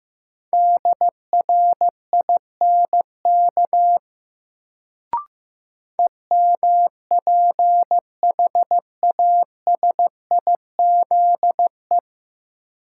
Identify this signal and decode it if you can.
Morse code, heard as a keyed tone, about 15 wpm, 700 Hz. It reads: DRINK EMPHASIZE
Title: Sentences from Top 1500 Words - New Word Drill Alt - 15wpm